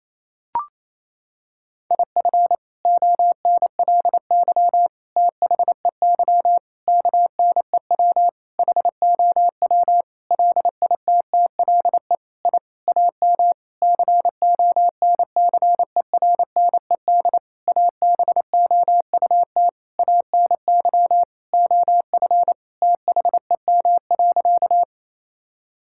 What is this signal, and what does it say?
Morse code, heard as a keyed tone, about 28 wpm, 700 Hz. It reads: IF ONLY THEY KNEW HOW LITTLE I AM CONCERNED ABOUT ANY OF THEM.
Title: Sentences from Top 1000 Words - Review Part 6 - 28wpm